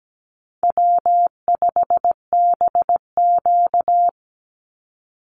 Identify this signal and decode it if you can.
Morse code, heard as a keyed tone, about 17 wpm, 700 Hz. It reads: W5BQ
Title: Call Signs Repeated 3x - 17wpm